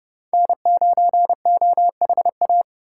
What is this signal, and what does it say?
Morse code, heard as a keyed tone, about 36 wpm, 700 Hz. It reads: N9OHA
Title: Speed Racing - USA 1x3 Call Signs 30wpm